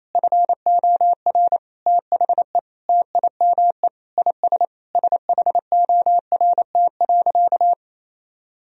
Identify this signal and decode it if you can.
Morse code, heard as a keyed tone, about 28 wpm, 700 Hz. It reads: FOR THE TIME IS SHORT.
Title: Sentences from Top 1000 Words - Review Part 2 - 28wpm